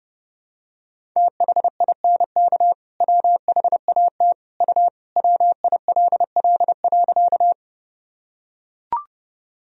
Morse code, heard as a keyed tone, about 32 wpm, 700 Hz. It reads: THINK WHAT U WILL.
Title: Sentences from Top 200 Words and Abbreviations 30wpm